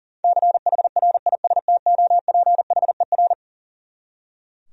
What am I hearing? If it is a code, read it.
Morse code, heard as a keyed tone, about 37 wpm, 700 Hz. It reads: CHRISTOPHER
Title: USA Common Names - 40wpm